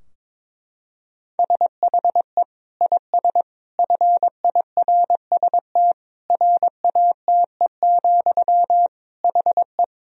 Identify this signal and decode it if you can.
Morse code, heard as a keyed tone, about 21 wpm, 700 Hz. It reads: SHE IS FIRST RATE, HE
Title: Sentences from Top 1000 Words - Review Part 4 - 22wpm